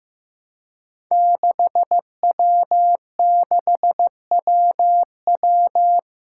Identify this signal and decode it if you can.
Morse code, heard as a keyed tone, about 15 wpm, 700 Hz. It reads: 6W6WW
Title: Five Challenging Letter-Number Combinations 15wpm